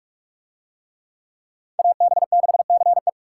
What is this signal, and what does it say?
Morse code, heard as a keyed tone, about 43 wpm, 700 Hz. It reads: AB6CE